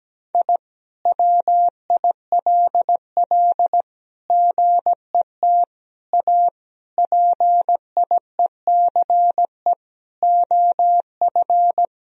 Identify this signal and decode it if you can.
Morse code, heard as a keyed tone, about 17 wpm, 700 Hz. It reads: I WILL GET A PIECE OF
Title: Sentences from Top 1000 Words - Review Part 2 - 17wpm